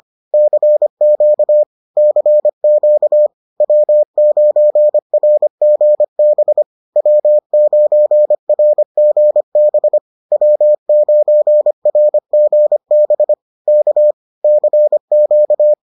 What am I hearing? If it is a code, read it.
Morse code, heard as a keyed tone, about 25 wpm, 600 Hz. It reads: CQ CQ W9RGB W9RGB W9RGB K CQ